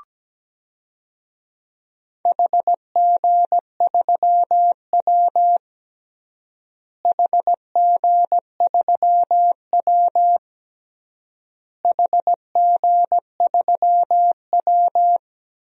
Morse code, heard as a keyed tone, about 17 wpm, 700 Hz. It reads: HG3W HG3W HG3W
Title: DXCC-100 Call Signs - 4 Characters - Repeated 3x - 17wpm